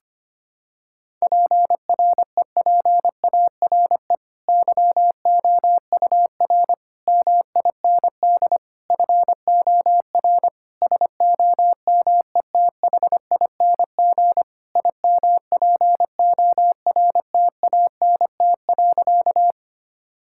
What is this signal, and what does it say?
Morse code, heard as a keyed tone, about 25 wpm, 700 Hz. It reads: PREPARE YOUR MIND FOR SOMETHING IMPORTANT.
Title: Sentences from Top 1000 Words - Review Part 6 - 25wpm